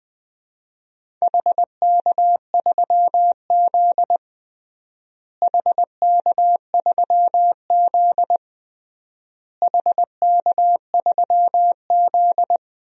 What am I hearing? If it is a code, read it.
Morse code, heard as a keyed tone, about 21 wpm, 700 Hz. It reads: HK3Z HK3Z HK3Z
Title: DXCC-100 Call Signs - 4 Characters - Repeated 3x - 20wpm